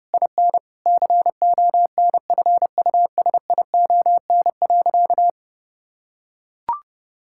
Morse code, heard as a keyed tone, about 32 wpm, 700 Hz. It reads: IN CONFUSION.
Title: Sentences from Top 1600 Words 30wpm